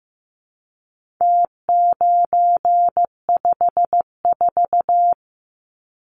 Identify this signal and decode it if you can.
Morse code, heard as a keyed tone, about 15 wpm, 700 Hz. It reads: T954
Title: Numbers - Contest Style - 4-Digit 15wpm